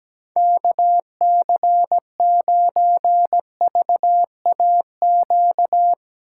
Morse code, heard as a keyed tone, about 20 wpm, 700 Hz. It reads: KC9VAQ
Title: Speed Racing - USA 2x3 Call Signs 17wpm